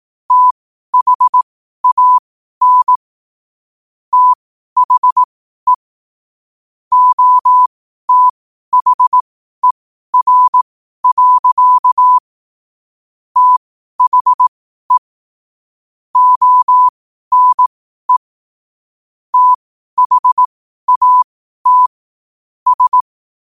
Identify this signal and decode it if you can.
Morse code, heard as a keyed tone, about 18 wpm, spaced as for 12 wpm, 1000 Hz. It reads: THAN THE OTHER. THE ONE THAT S
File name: qotdfiles / 12wpm